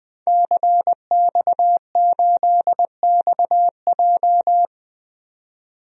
Morse code, heard as a keyed tone, about 20 wpm, 700 Hz. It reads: CX8XJ
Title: DXCC-100 Call Signs - 5 Characters - 20wpm